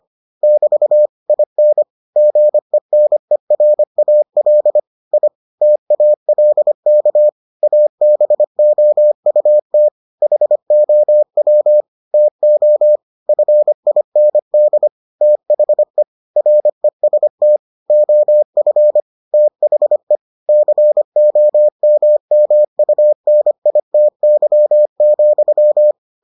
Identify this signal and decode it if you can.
Morse code, heard as a keyed tone, about 25 wpm, 600 Hz. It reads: <BT> IN GENERAL I TALK ABOUT HOW TO FIND THE REST OF THE COMMUNITY,